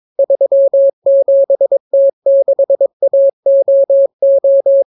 Morse code, heard as a keyed tone, about 22 wpm, 550 Hz. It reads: 37T6AOO